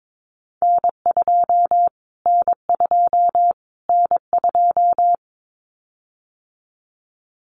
Morse code, heard as a keyed tone, about 22 wpm, 700 Hz. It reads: N2 N2 N2
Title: Callsign Prefixes for Top 25 DXCC - Repeated 3x - 22wpm